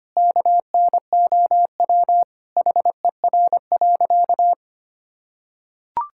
Morse code, heard as a keyed tone, about 25 wpm, 700 Hz. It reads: KNOW HER.
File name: practice / 25wpm / Sentences with 4-Word Phrases - 25wpm